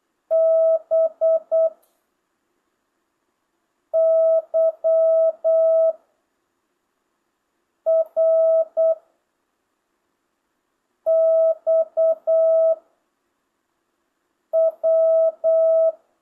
Morse code, heard as a keyed tone, about 8 wpm, 650 Hz. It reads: B Y R X W
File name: jh8cbh / 2025cw